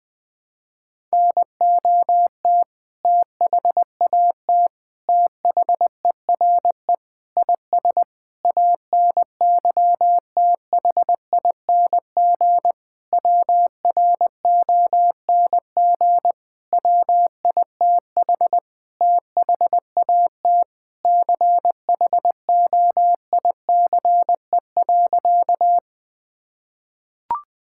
Morse code, heard as a keyed tone, about 20 wpm, 700 Hz. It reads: NOT THAT THERE IS ANYTHING WRONG WITH THAT CHOICE.
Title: Sentences with 7-Word Phrases - 20wpm